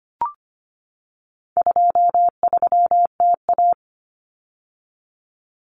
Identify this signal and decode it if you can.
Morse code, heard as a keyed tone, about 25 wpm, 700 Hz. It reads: 23TA
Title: Common Cut Numbers - 4-Digit 25wpm